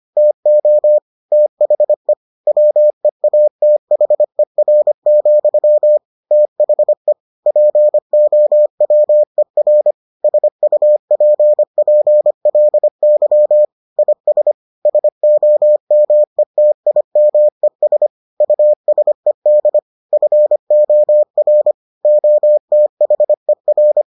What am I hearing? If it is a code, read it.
Morse code, heard as a keyed tone, about 25 wpm, 600 Hz. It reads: TO THE WEATHER, THE POWER SUPPLY IS SOMETIMES USED FOR OTHER